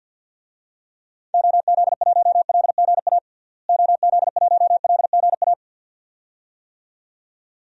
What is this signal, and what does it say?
Morse code, heard as a keyed tone, about 52 wpm, 700 Hz. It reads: OZ1LGA OZ1LGA
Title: DXCC-100 Call Signs - 6 Characters - Repeated 2x - 50wpm